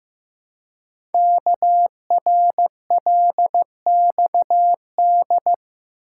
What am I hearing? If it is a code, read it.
Morse code, heard as a keyed tone, about 15 wpm, 700 Hz. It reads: KRLXD
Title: Code Groups - 5 Letters - 15wpm